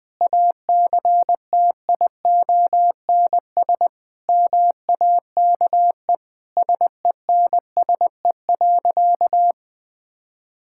Morse code, heard as a keyed tone, about 20 wpm, 700 Hz. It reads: ACTIONS MAKE SENSE.
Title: Sentences with 5-Word Phrases - Spoken Beginning - 20wpm